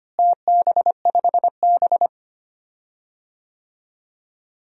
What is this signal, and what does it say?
Morse code, heard as a keyed tone, about 25 wpm, 700 Hz. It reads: TB5B
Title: DXCC-100 Call Signs - 4 Characters - Repeated 2x - 25wpm